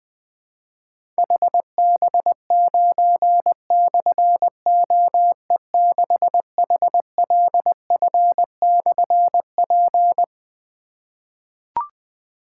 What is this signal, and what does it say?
Morse code, heard as a keyed tone, about 20 wpm, 700 Hz. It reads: HB9/OE6HLF/P
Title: Difficult Call Signs 20wpm